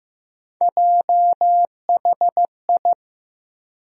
Morse code, heard as a keyed tone, about 15 wpm, 700 Hz. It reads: JHI
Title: USA Call Sign Postfixes - 3 Letters 15wpm